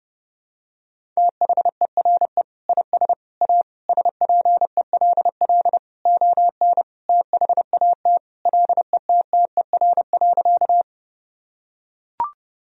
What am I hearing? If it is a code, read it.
Morse code, heard as a keyed tone, about 32 wpm, 700 Hz. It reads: THERE IS A SPELL ON THAT LETTER.
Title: Sentences from Top 1000 Words - Review Part 1 - 30wpm